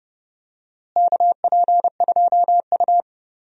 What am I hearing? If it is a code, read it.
Morse code, heard as a keyed tone, about 30 wpm, 700 Hz. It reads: KP2U
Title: DXCC-100 Call Signs - 4 Characters - Repeated 2x - 30wpm